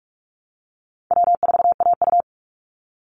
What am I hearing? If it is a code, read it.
Morse code, heard as a keyed tone, about 44 wpm, 700 Hz. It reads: W4AU
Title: Easy Call Signs Repeated 2x - 45wpm